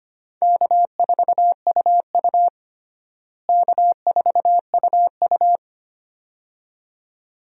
Morse code, heard as a keyed tone, about 25 wpm, 700 Hz. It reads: K4UU K4UU
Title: USA Call Signs - 4 Characters - Repeated 2x - 25wpm